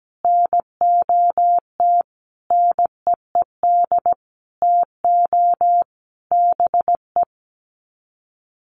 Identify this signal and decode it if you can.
Morse code, heard as a keyed tone, about 17 wpm, 700 Hz. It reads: NOT NEED TO BE
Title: Sets of 5 Words Encore 17wpm